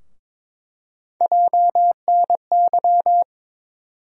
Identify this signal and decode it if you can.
Morse code, heard as a keyed tone, about 22 wpm, 700 Hz. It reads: JNY
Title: USA Call Sign Postfixes - 3 Letters 22wpm